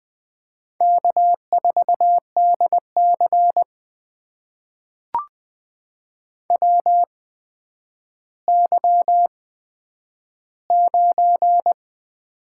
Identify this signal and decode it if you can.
Morse code, heard as a keyed tone, about 20 wpm, spaced as for 10 wpm, 700 Hz. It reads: K4DC WY9
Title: USA Call Signs - 8x Character Spacing - 20wpm